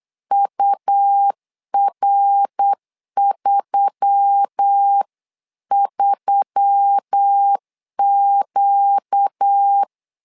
Morse code, heard as a keyed tone, about 8 wpm, 800 Hz. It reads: UR33Q
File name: LEZIONI / 4 Q1S3 8wpm 3037